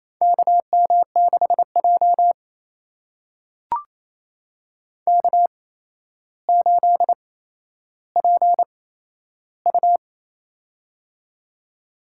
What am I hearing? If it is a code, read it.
Morse code, heard as a keyed tone, about 28 wpm, spaced as for 12 wpm, 700 Hz. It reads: KM6J K8PU
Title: USA Call Signs - 4 Characters - 8x Character Spacing - 28wpm